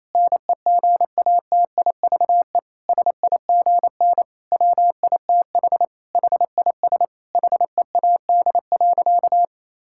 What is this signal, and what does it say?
Morse code, heard as a keyed tone, about 27 wpm, 700 Hz. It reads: NEGATIVE SIGN WITH HIS HEAD.
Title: Sentences from Top 1000 Words - Review Part 6 - 28wpm